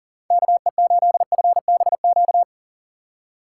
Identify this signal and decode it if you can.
Morse code, heard as a keyed tone, about 38 wpm, 700 Hz. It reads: KE8FBQ
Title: USA Call Signs - 6 Characters - 2x Character Spacing - 40wpm